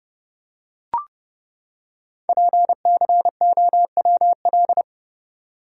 Morse code, heard as a keyed tone, about 30 wpm, 690 Hz. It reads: PCOWL
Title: Code Groups - 5 Letters - 30wpm